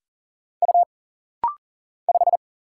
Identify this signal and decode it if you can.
Morse code, heard as a keyed tone, about 40 wpm, 700 Hz. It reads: U 5
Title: Single Letter-Number - Rapid-Fire 40wpm